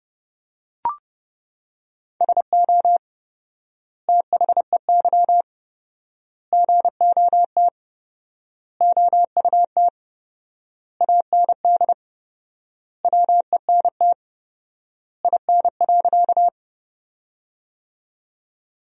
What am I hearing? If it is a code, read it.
Morse code, heard as a keyed tone, about 31 wpm, 700 Hz. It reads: SO THEY GOT OUT AND WENT IN.
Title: 4x Word-Spacing - Sentences from Top 300 Words - 30wpm